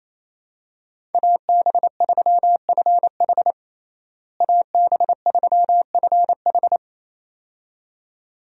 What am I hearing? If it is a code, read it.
Morse code, heard as a keyed tone, about 28 wpm, 700 Hz. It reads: AB3FH AB3FH
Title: USA Call Signs - 5 Characters - Repeated 2x - 28wpm